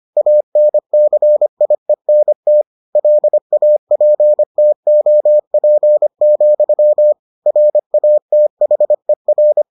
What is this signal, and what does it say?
Morse code, heard as a keyed tone, about 25 wpm, 600 Hz. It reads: ANCIENT LAPTOP, RATHER